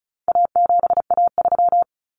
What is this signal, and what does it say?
Morse code, heard as a keyed tone, about 34 wpm, 700 Hz. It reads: A7A3
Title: Common Cut Numbers - 4-Digit 35wpm